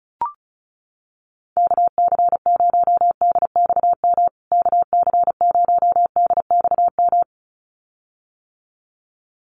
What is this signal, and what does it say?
Morse code, heard as a keyed tone, about 34 wpm, 700 Hz. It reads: KC0DXM KC0DXM
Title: USA Call Signs Repeated 2x 35wpm